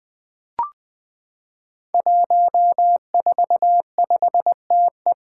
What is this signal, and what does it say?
Morse code, heard as a keyed tone, about 20 wpm, 700 Hz. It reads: 145TE
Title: Five Challenging Letter-Number Combinations 20wpm